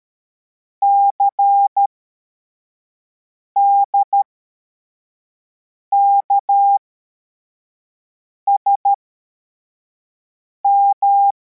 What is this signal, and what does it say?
Morse code, heard as a keyed tone, about 13 wpm, spaced as for 4 wpm, 800 Hz. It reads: CDKSM
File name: formation / cw / Text55